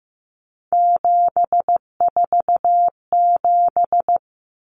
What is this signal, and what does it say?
Morse code, heard as a keyed tone, about 15 wpm, 700 Hz. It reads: 747
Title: Common Cut Numbers - 3-Digit 15wpm